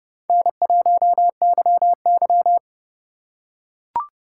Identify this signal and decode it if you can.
Morse code, heard as a keyed tone, about 29 wpm, 700 Hz. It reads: N1YY